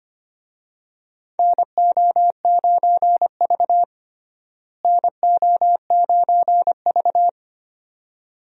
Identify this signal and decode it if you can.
Morse code, heard as a keyed tone, about 25 wpm, 700 Hz. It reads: NO9V NO9V
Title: USA Call Signs - 4 Characters - Repeated 2x - 25wpm